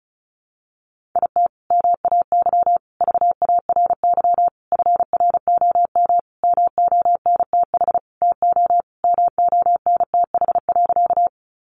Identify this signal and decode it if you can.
Morse code, heard as a keyed tone, about 34 wpm, 700 Hz. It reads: IT MAY VARY FROM MONTH TO MONTH.
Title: Sentences from Top 1200 Words 35wpm